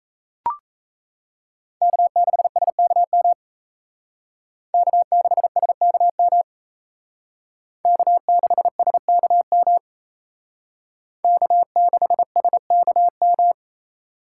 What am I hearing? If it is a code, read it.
Morse code, heard as a keyed tone, about 35 wpm, 700 Hz. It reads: K6SKM K6SKM K6SKM K6SKM